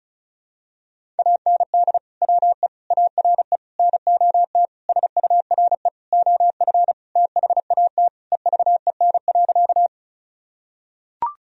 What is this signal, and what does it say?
Morse code, heard as a keyed tone, about 34 wpm, 700 Hz. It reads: AND WE ARE NOT SURE OF THAT EVEN.
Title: Sentences from Top 1000 Words - Review Part 1 - 35wpm